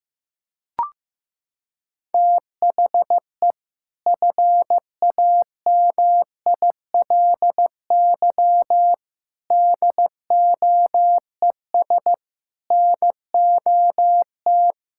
Morse code, heard as a keyed tone, about 15 wpm, 700 Hz. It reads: THE FAMILY DOES NOT
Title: Sentences from Top 1600 Words 15wpm